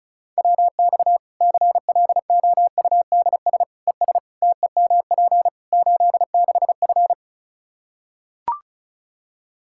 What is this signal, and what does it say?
Morse code, heard as a keyed tone, about 36 wpm, 700 Hz. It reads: WX CLOUDS ES TEMP 86F
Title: Practice QSO - Exchange 2 - Info - 35wpm